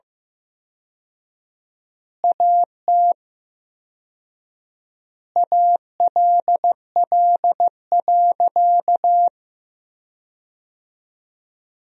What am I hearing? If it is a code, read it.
Morse code, heard as a keyed tone, about 15 wpm, 700 Hz. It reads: AT ALL.
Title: 4x Word-Spacing - Sentences from Top 500 Words - 15wpm